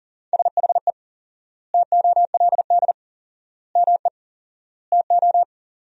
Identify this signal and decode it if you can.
Morse code, heard as a keyed tone, about 40 wpm, 700 Hz. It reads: SHE TOLD ME TO
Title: 4x Word-Spacing - Sentences from Top 400 Words - 40wpm